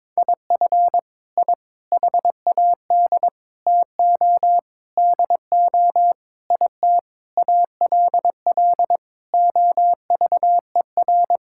Morse code, heard as a keyed tone, about 22 wpm, 700 Hz. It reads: IF I HAD TO DO IT ALL OVER
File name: Sentences with 9-Word Phrases - 22wpm